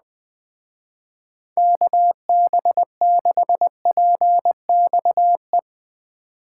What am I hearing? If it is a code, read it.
Morse code, heard as a keyed tone, about 19 wpm, 700 Hz. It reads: KB6PXE
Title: Speed Racing - USA Call Signs 17wpm